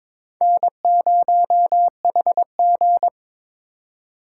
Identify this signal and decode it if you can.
Morse code, heard as a keyed tone, about 22 wpm, 700 Hz. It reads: N0HG
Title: USA Call Signs - 4 Characters - 4x Character Spacing - 22wpm